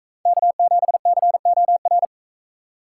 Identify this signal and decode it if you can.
Morse code, heard as a keyed tone, about 42 wpm, 700 Hz. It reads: K7COR